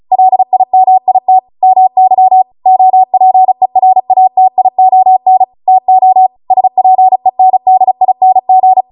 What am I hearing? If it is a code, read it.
Morse code, heard as a keyed tone, about 36 wpm, 750 Hz. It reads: LIMIT MY OPERATION TO SPENDING
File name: Morse / 35WPM